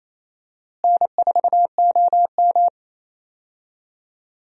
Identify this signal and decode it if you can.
Morse code, heard as a keyed tone, about 28 wpm, 700 Hz. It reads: N4OM
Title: USA Call Signs - 4 Characters - Repeated 2x - 28wpm